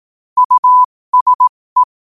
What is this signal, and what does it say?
Morse code, heard as a keyed tone, about 18 wpm, spaced as for 15 wpm, 1000 Hz. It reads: USE
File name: qotdfiles / 15wpm